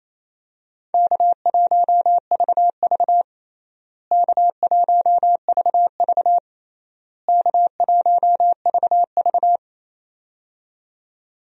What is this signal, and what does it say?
Morse code, heard as a keyed tone, about 28 wpm, 700 Hz. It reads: K1VV K1VV K1VV
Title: USA Call Signs - 4 Characters - Repeated 3x - 28wpm